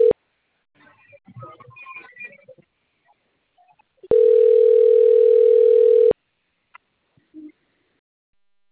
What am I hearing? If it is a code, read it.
Ringback tone (North America), 440 + 480 Hz.